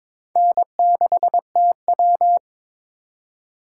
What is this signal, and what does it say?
Morse code, heard as a keyed tone, about 22 wpm, 700 Hz. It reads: N6TW
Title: K1USN SST QSO Practice - 22wpm